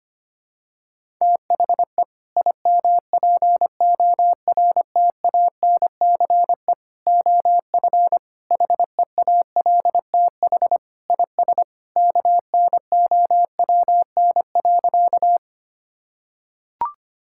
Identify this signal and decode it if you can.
Morse code, heard as a keyed tone, about 25 wpm, 700 Hz. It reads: THE IMPORTANCE OF HEALTH IS KNOWN.
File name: practice / 25wpm / Sentences with 3-Word Phrases - 25wpm